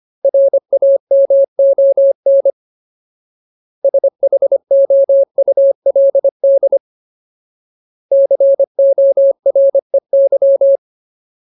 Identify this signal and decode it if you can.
Morse code, heard as a keyed tone, about 25 wpm, 550 Hz. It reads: RAMON SHOULD COREY
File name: Ultimate Word List / 25WPM Ultimate